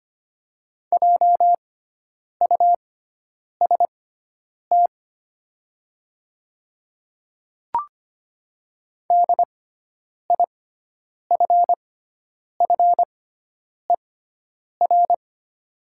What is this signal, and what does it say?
Morse code, heard as a keyed tone, about 25 wpm, spaced as for 9 wpm, 700 Hz. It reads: JUST DIFFER
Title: 6x Character Spacing - Top 200 Words - 25wpm